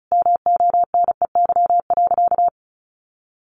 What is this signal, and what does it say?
Morse code, heard as a keyed tone, about 34 wpm, 700 Hz. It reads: MONEY.
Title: Sentences from Top 1400 Words 35wpm